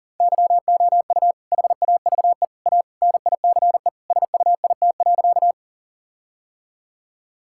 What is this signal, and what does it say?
Morse code, heard as a keyed tone, about 40 wpm, 700 Hz. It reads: YOU HAVE A NICE SUIT.
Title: Sentences from Top 1600 Words 40wpm